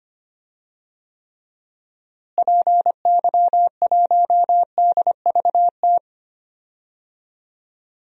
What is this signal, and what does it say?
Morse code, heard as a keyed tone, about 25 wpm, 700 Hz. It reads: PY1DVT
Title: DXCC-100 Call Signs - 6 Characters - Repeated 3x - 25wpm